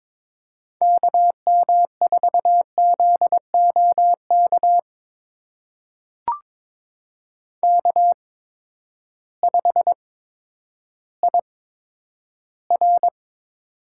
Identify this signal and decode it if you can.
Morse code, heard as a keyed tone, about 21 wpm, spaced as for 11 wpm, 700 Hz. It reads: KM4ZOK K5IR
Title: USA Call Signs - 8x Character Spacing - 22wpm